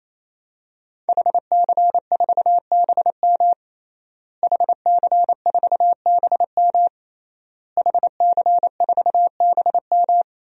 Morse code, heard as a keyed tone, about 28 wpm, 700 Hz. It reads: HC4BM HC4BM HC4BM
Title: DXCC-100 Call Signs - 5 Characters - Repeated 3x - 28wpm